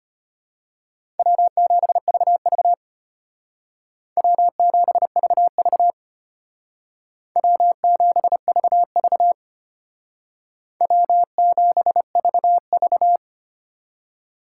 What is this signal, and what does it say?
Morse code, heard as a keyed tone, about 33 wpm, 700 Hz. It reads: W7VV W7VV W7VV W7VV